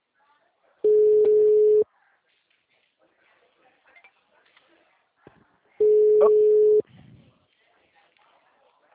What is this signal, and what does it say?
Ringback tone (Europe, 425 Hz), with speech.
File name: out- -2004